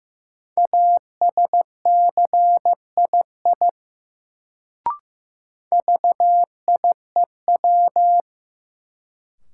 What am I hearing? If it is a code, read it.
Morse code, heard as a keyed tone, about 15 wpm, 700 Hz. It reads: ASCII VIEW
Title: Sets of 5 Words - New Word Drill - 15wpm